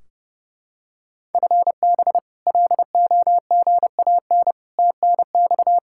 Morse code, heard as a keyed tone, about 30 wpm, 700 Hz. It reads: FB LOGAN TNX
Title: K1USN SST QSO Practice 30wpm